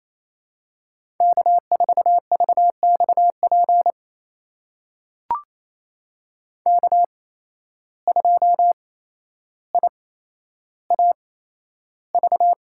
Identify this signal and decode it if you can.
Morse code, heard as a keyed tone, about 28 wpm, spaced as for 12 wpm, 700 Hz. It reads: K4VXP K2IAV
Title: USA Call Signs - 5 Characters - 8x Character Spacing - 28wpm